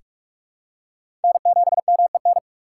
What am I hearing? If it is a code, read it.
Morse code, heard as a keyed tone, about 44 wpm, 700 Hz. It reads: N7MEN
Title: USA Call Signs - 5 Characters - 6x Character Spacing - 45wpm